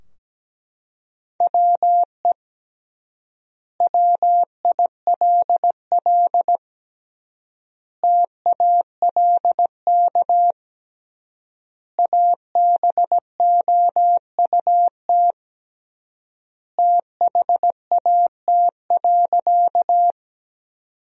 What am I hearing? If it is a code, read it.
Morse code, heard as a keyed tone, about 17 wpm, 700 Hz. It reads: WE WILL TALK ABOUT THAT.
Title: 3x Word-Spacing - Sentences from Top 400 Words - 17wpm